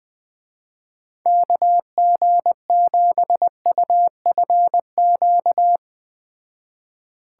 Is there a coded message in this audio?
Yes: Morse code from 1.3 to 5.8 s, heard as a keyed tone, about 20 wpm, 700 Hz. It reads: KG7UFQ